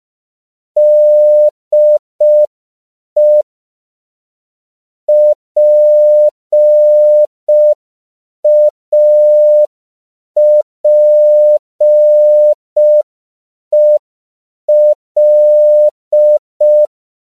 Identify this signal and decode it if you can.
Morse code, heard as a keyed tone, about 5 wpm, 600 Hz. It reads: DE PAPEL